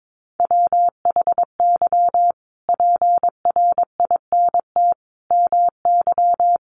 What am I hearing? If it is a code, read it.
Morse code, heard as a keyed tone, about 22 wpm, 700 Hz. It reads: WHY PRINT MY
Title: Sentences from Top 1400 Words 22wpm